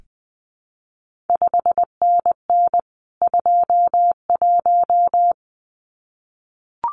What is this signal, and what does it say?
Morse code, heard as a keyed tone, about 20 wpm, 700 Hz. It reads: 5NN 21
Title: CQ WW DX Contest Exchange Encore - 20wpm